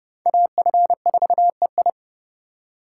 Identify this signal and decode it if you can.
Morse code, heard as a keyed tone, about 29 wpm, 700 Hz. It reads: AF4EI
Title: USA Call Signs - 5 Characters - 8x Character Spacing - 30wpm